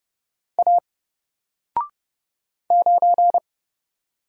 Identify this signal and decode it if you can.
Morse code, heard as a keyed tone, about 30 wpm, 700 Hz. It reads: A 9
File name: Single-Letter-Number-30wpm